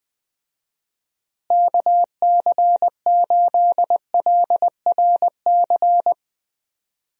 Morse code, heard as a keyed tone, about 20 wpm, 700 Hz. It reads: KC8LRC
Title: USA Call Signs - 6 Characters - Repeated 3x - 20wpm